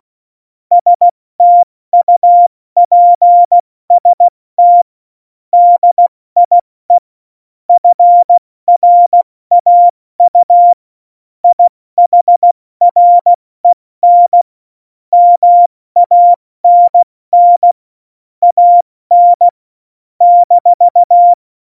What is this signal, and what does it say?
Morse code, heard as a keyed tone, about 16 wpm, 700 Hz. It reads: STUPST DIE FRAU IHREN MANN AN -
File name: Morsekurs / Witze / Witz 04. Juni 2019, 70 F80